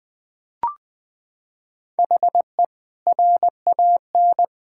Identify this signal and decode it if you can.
Morse code, heard as a keyed tone, about 21 wpm, 700 Hz. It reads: HE RAN
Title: Sentences from Top 1000 Words - Review Part 6 - 20wpm